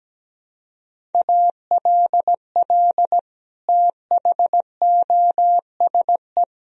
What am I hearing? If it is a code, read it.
Morse code, heard as a keyed tone, about 17 wpm, 700 Hz. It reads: ALL THOSE
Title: Sentences from Top 1000 Words - Review Part 1 - 17wpm